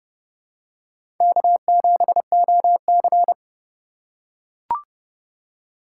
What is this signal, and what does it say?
Morse code, heard as a keyed tone, about 30 wpm, 700 Hz. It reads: K7OC
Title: USA Call Signs - 4 Characters - 4x Character Spacing - 30wpm